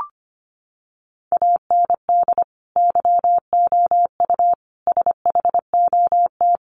Morse code, heard as a keyed tone, about 25 wpm, 700 Hz. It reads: AND YOU SHOT
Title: Sentences from Top 1500 Words 25wpm